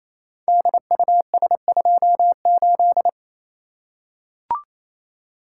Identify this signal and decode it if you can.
Morse code, heard as a keyed tone, about 28 wpm, 700 Hz. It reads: DUS28
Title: Code Groups - 5 Letter:Numbers - 28wpm